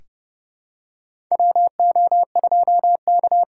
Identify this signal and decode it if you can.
Morse code, heard as a keyed tone, about 30 wpm, 700 Hz. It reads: WO2K
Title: USA Call Signs - 6x Character Spacing - 30wpm